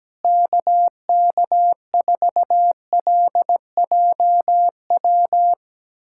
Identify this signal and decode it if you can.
Morse code, heard as a keyed tone, about 17 wpm, 700 Hz. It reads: KK4LJW
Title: USA Callsign DE Callsign 17wpm